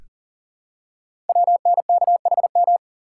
Morse code, heard as a keyed tone, about 39 wpm, 690 Hz. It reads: WNKHM
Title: Code Groups - 5 Letters - 40wpm